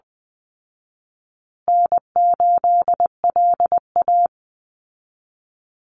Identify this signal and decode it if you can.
Morse code, heard as a keyed tone, about 20 wpm, 700 Hz. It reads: N8LA
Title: Easy Call Signs Repeated 3x - 20wpm